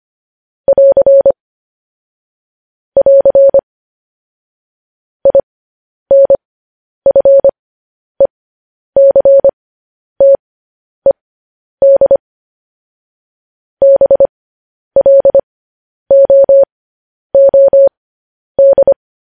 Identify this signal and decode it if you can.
Morse code, heard as a keyed tone, about 25 wpm, spaced as for 10 wpm, 550 Hz. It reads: <AR> <AR> INFECTED BLOOD
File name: podcast / News headlines-1x-25x10